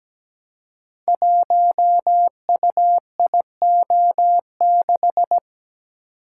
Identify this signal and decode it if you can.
Morse code, heard as a keyed tone, about 17 wpm, 700 Hz. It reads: 1UIO6
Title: Code Groups - 5 Letter:Numbers - 17wpm